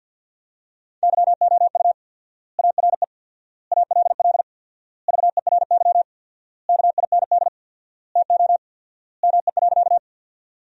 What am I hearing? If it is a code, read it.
Morse code, heard as a keyed tone, about 52 wpm, 700 Hz. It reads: YOU ARE ALL VERY KIND TO ME.
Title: 4x Word-Spacing - Sentences from Top 200 Words - 50wpm